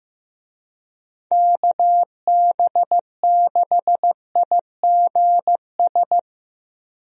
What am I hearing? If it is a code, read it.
Morse code, heard as a keyed tone, about 15 wpm, 700 Hz. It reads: KB6IGS
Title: USA Call Signs - 6 Characters - 15wpm